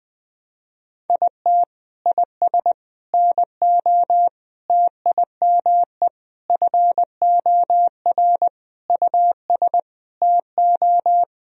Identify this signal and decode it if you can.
Morse code, heard as a keyed tone, about 20 wpm, 700 Hz. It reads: IT IS NO TIME FOR US TO
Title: Sentences from Top 1000 Words - Review Part 3 - 20wpm